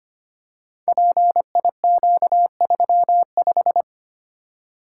Morse code, heard as a keyed tone, about 25 wpm, 700 Hz. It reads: PIQ35
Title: Code Groups - 5 Letter:Numbers - 25wpm